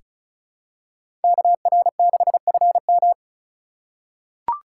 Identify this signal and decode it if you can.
Morse code, heard as a keyed tone, about 34 wpm, 700 Hz. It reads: KR6FM
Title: USA Call Signs - 8x Character Spacing - 35wpm